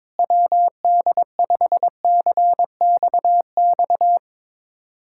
Morse code, heard as a keyed tone, about 22 wpm, 700 Hz. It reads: WD5CXX
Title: USA Callsign DE Callsign 22wpm